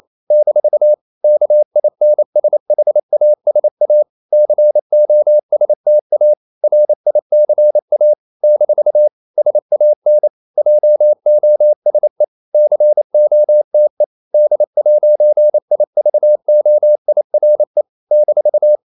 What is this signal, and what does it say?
Morse code, heard as a keyed tone, about 28 wpm, 600 Hz. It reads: - KINSHASA COSTA RICA - SAN JOSE COTE D'IVOIRE -
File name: CW / Practice / NationalCapitals28